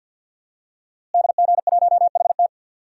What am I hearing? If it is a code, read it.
Morse code, heard as a keyed tone, about 48 wpm, 700 Hz. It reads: DG1HT
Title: DXCC-100 Call Signs - 5 Characters - Repeated 3x - 50wpm